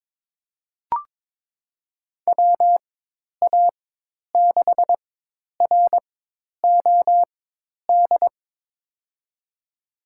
Morse code, heard as a keyed tone, about 22 wpm, spaced as for 10 wpm, 700 Hz. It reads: WA6ROD
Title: USA Call Signs - 4x Character Spacing - 22wpm